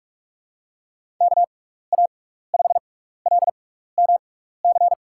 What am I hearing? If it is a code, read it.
Morse code, heard as a keyed tone, about 44 wpm, spaced as for 15 wpm, 700 Hz. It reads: KA5LMC
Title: USA Call Signs - 6 Characters - 6x Character Spacing - 45wpm